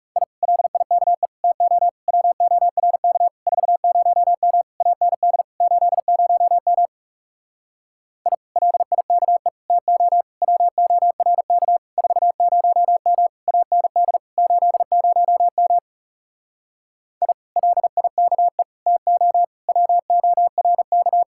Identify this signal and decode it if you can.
Morse code, heard as a keyed tone, about 37 wpm, 700 Hz. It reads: I LIKE TO WORK 40M AND 80M I LIKE TO WORK 40M AND 80M I LIKE TO WORK
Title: Speed-Racing Common QSO Phrases - 30wpm